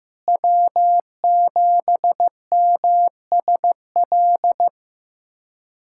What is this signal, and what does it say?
Morse code, heard as a keyed tone, about 15 wpm, 700 Hz. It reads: W7MSL
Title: K1USN SST QSO Practice - 15wpm